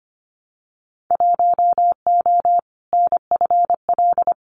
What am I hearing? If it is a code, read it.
Morse code, heard as a keyed tone, about 25 wpm, 700 Hz. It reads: 1O NFL
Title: Winter Field Day Exchanges 25wpm